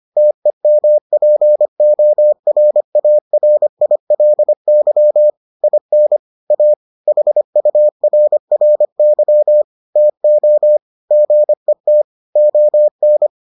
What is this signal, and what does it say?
Morse code, heard as a keyed tone, about 25 wpm, 600 Hz. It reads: TEMPORARILY IN A HURRY TO GET ON